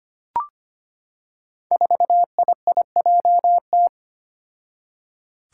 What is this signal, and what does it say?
Morse code, heard as a keyed tone, about 25 wpm, 700 Hz. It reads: 4IIJT
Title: Five Challenging Letter-Number Combinations 25wpm